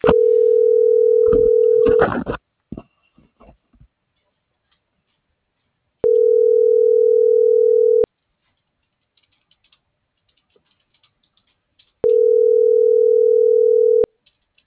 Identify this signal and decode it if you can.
Ringback tone (North America), 440 + 480 Hz.